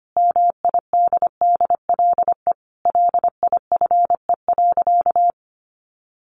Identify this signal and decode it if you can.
Morse code, heard as a keyed tone, about 25 wpm, 700 Hz. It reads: MIDDLE LIFE.
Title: Sentences from Top 1300 Words 25wpm